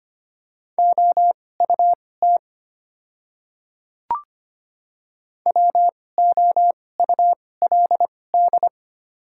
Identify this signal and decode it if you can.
Morse code, heard as a keyed tone, about 25 wpm, spaced as for 18 wpm, 700 Hz. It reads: OUT WOULD
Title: 2x Character Spacing - Top 100 Words - 25wpm